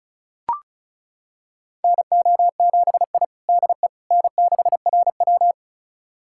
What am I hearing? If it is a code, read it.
Morse code, heard as a keyed tone, about 36 wpm, 700 Hz. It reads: NO7I DE N6RW
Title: USA Callsign DE Callsign - 4 Characters - 35wpm